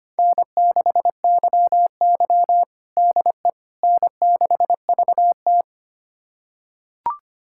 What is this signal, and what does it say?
Morse code, heard as a keyed tone, about 25 wpm, 700 Hz. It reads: N6YY DE N6VT
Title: USA Callsign DE Callsign - 4 Characters - 25wpm